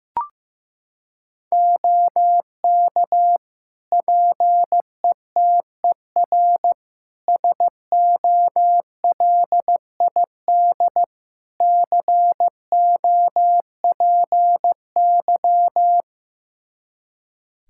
Morse code, heard as a keyed tone, about 15 wpm, 700 Hz. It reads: OK PETER SOLID COPY
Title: K1USN SST QSO Practice 15wpm